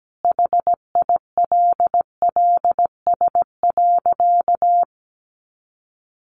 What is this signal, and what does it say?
Morse code, heard as a keyed tone, about 17 wpm, 700 Hz. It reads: HILLS.
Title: A Sound Of Thunder 17wpm